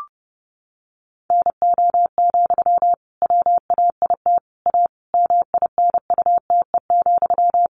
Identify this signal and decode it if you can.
Morse code, heard as a keyed tone, about 30 wpm, 700 Hz. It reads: NO, WAIT A MINUTE,
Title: Sentences from Top 1500 Words 30wpm